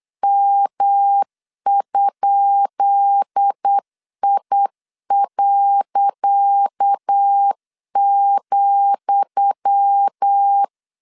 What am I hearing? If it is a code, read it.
Morse code, heard as a keyed tone, about 8 wpm, 800 Hz. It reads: M?I.,
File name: LEZIONI / 10 SIGNS / 10D COMA 8wpm 3749